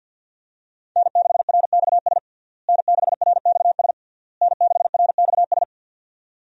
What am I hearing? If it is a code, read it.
Morse code, heard as a keyed tone, about 51 wpm, 700 Hz. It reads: N6RXS N6RXS N6RXS
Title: USA Call Signs - 5 Characters - Repeated 3x - 50wpm